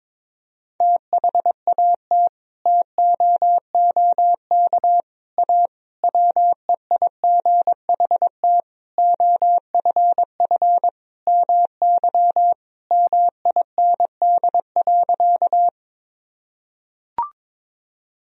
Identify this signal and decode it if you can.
Morse code, heard as a keyed tone, about 22 wpm, 700 Hz. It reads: THAT TOOK A WEIGHT OFF MY MIND.
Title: Sentences from Top 1000 Words - Review Part 3 - 22wpm